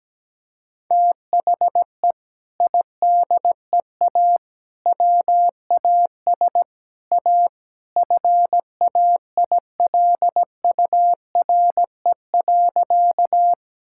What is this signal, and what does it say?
Morse code, heard as a keyed tone, about 17 wpm, 700 Hz. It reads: THE IDEA WAS A FAILURE.
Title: Sentences from Top 1000 Words - Review Part 6 - 17wpm